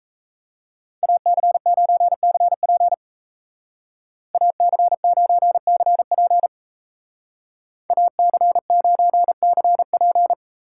Morse code, heard as a keyed tone, about 33 wpm, 700 Hz. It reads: AC9CP AC9CP AC9CP
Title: Speed Racing - USA Call Signs 28wpm